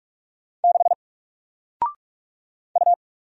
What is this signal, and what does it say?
Morse code, heard as a keyed tone, about 44 wpm, 700 Hz. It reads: 6 U
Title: Single Letter-Number Alt 45wpm